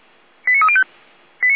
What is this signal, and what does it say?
Five-tone selective call (ZVEI), heard as quick runs of five tones, tones of about 70 ms. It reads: 89305
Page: SDR/ZVEI